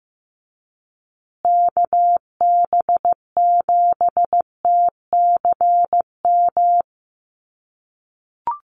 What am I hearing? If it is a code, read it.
Morse code, heard as a keyed tone, about 15 wpm, 700 Hz. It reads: KB7TCM
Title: USA Call Signs Repeated 2x 15wpm